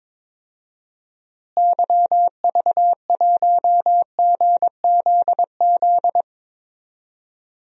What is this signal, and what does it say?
Morse code, heard as a keyed tone, about 22 wpm, 700 Hz. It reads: YV1GZZ
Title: DXCC-100 Call Signs - 6 Characters - Repeated 2x - 22wpm